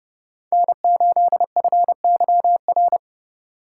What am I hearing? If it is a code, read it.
Morse code, heard as a keyed tone, about 29 wpm, 700 Hz. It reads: N8FYR